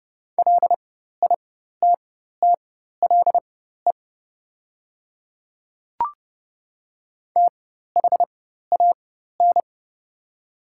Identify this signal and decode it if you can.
Morse code, heard as a keyed tone, about 30 wpm, spaced as for 14 wpm, 700 Hz. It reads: LITTLE THAN
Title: 4x Character Spacing - Top 300 Words - 30wpm